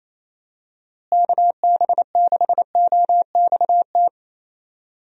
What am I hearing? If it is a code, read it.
Morse code, heard as a keyed tone, about 27 wpm, 700 Hz. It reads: KB6OXT
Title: USA Call Signs - 6 Characters - Repeated 2x - 28wpm